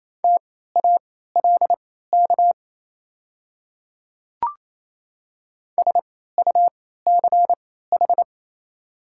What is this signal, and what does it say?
Morse code, heard as a keyed tone, about 28 wpm, spaced as for 16 wpm, 700 Hz. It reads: TALK SUCH
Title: 3x Character Spacing - Top 500 Words - 28wpm